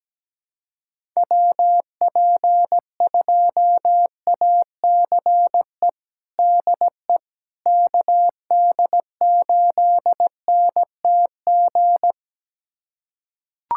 Morse code, heard as a keyed tone, about 17 wpm, 700 Hz. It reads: WP2ACE DE KD8NTG
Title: USA Callsign DE Callsign - 6 Characters - 17wpm